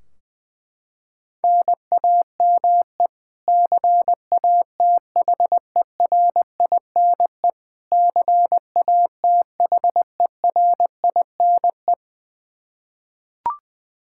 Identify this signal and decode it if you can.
Morse code, heard as a keyed tone, about 20 wpm, 700 Hz. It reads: NAME CATHERINE CATHERINE
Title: Practice QSO - Exchange 1 - Beginning - 20wpm